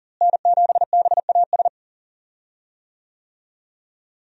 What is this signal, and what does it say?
Morse code, heard as a keyed tone, about 39 wpm, 700 Hz. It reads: N7BAS
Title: USA Call Signs - 5 Characters - Repeated 2x - 40wpm